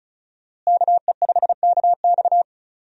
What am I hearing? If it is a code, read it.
Morse code, heard as a keyed tone, about 36 wpm, 700 Hz. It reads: KE5KX
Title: USA Call Signs - 5 Characters - Repeated 3x - 35wpm